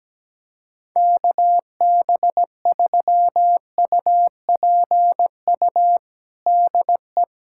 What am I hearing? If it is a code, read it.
Morse code, heard as a keyed tone, about 17 wpm, 700 Hz. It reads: KB3UPU DE
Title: USA Callsign DE Callsign 17wpm